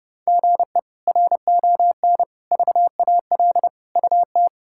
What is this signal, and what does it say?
Morse code, heard as a keyed tone, about 31 wpm, 700 Hz. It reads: GE RON VAL UT
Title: K1USN SST QSO Practice - 30wpm